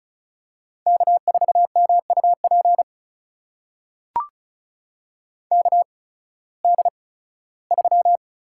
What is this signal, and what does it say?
Morse code, heard as a keyed tone, about 36 wpm, spaced as for 19 wpm, 700 Hz. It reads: K4MUP KD3
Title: USA Call Signs - 5 Characters - 8x Character Spacing - 35wpm